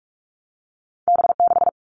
Morse code, heard as a keyed tone, about 46 wpm, 700 Hz. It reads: B6
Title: All Cut Numbers - 2-Digit 45wpm